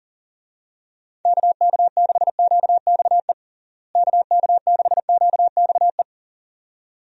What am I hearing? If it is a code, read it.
Morse code, heard as a keyed tone, about 40 wpm, 700 Hz. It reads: KK6QXE KK6QXE
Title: USA Call Signs - 6 Characters - Repeated 2x - 40wpm